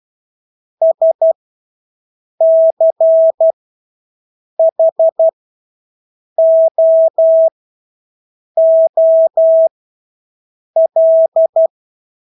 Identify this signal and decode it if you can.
Morse code, heard as a keyed tone, about 12 wpm, spaced as for 6 wpm, 650 Hz. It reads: SCHOOL
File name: Engelske ord 12-6 wpm